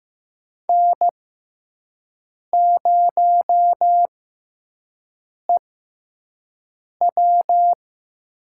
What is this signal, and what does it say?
Morse code, heard as a keyed tone, about 15 wpm, spaced as for 5 wpm, 700 Hz. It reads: N0EW
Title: USA Call Signs - 6x Character Spacing - 15wpm